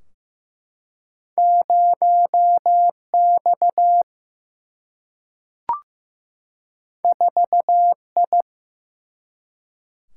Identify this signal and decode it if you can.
Morse code, heard as a keyed tone, about 15 wpm, 700 Hz. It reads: 0X 4I
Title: Single-Letter Single-Number 15wpm